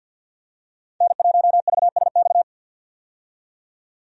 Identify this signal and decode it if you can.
Morse code, heard as a keyed tone, about 52 wpm, 700 Hz. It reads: N1VSX
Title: USA Call Signs - 5 Characters - 50wpm